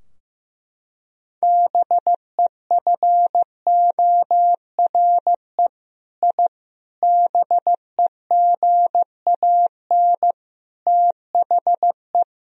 Morse code, heard as a keyed tone, about 15 wpm, 700 Hz. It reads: BEFORE I BEGAN THE
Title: Sentences from Top 1000 Words - Review Part 4 - 15wpm